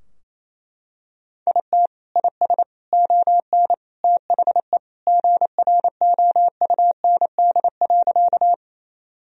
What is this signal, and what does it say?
Morse code, heard as a keyed tone, about 28 wpm, 700 Hz. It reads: IT IS ON THE GROUND.
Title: Sentences from Top 1000 Words - Review Part 2 - 28wpm